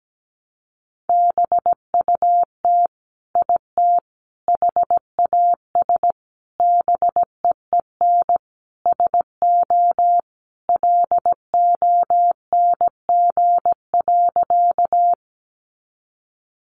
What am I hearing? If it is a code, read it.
Morse code, heard as a keyed tone, about 17 wpm, 700 Hz. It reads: BUT IT HAS BEEN SO LONG.
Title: Sentences from Top 100 Words 17wpm